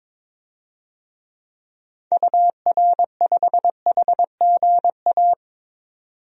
Morse code, heard as a keyed tone, about 21 wpm, 700 Hz. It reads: UR5HGA
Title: DXCC-100 Call Signs - 6 Characters - 22wpm